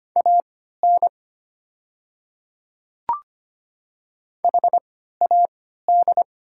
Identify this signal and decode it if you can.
Morse code, heard as a keyed tone, about 25 wpm, spaced as for 14 wpm, 700 Hz. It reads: AN HAD
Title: 3x Character Spacing - Top 100 Words - 25wpm